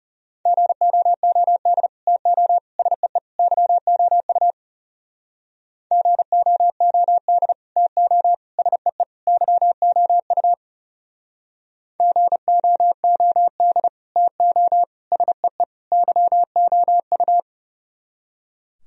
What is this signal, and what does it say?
Morse code, heard as a keyed tone, about 38 wpm, 700 Hz. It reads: GOOD TO SEE YOU GOOD TO SEE YOU GOOD TO SEE YOU